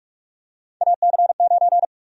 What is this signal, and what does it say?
Morse code, heard as a keyed tone, about 46 wpm, 700 Hz. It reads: AC9
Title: USA Call Sign Prefixes - 45wpm